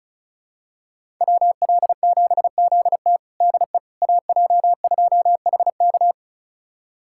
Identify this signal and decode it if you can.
Morse code, heard as a keyed tone, about 36 wpm, 700 Hz. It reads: WL7ZT DE AJ2HK
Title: USA Callsign DE Callsign - 5 Characters - 35wpm